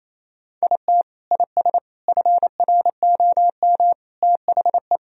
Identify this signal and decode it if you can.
Morse code, heard as a keyed tone, about 28 wpm, 700 Hz. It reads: IT IS FROM THE
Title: Sentences from Top 1000 Words - Review Part 1 - 28wpm